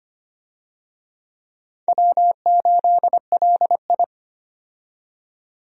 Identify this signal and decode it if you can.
Morse code, heard as a keyed tone, about 25 wpm, 700 Hz. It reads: W8LI
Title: USA Call Signs - 4 Characters - 25wpm